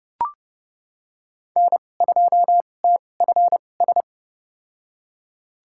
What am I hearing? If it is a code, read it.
Morse code, heard as a keyed tone, about 32 wpm, spaced as for 22 wpm, 700 Hz. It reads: N2TFS
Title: USA Call Signs - 5 Characters - 2x Character Spacing - 30wpm